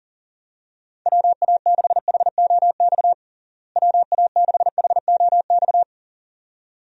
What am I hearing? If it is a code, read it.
Morse code, heard as a keyed tone, about 40 wpm, 700 Hz. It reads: WA6HOX WA6HOX
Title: USA Call Signs - 6 Characters - Repeated 2x - 40wpm